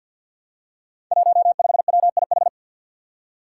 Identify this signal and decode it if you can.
Morse code, heard as a keyed tone, about 46 wpm, 700 Hz. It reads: 15WIH
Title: Five Challenging Letter-Number Combinations 50wpm